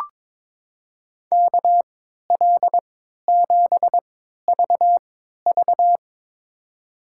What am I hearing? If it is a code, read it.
Morse code, heard as a keyed tone, about 22 wpm, spaced as for 12 wpm, 700 Hz. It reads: KL7VV
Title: USA Call Signs - 5 Characters - 3x Character Spacing - 22wpm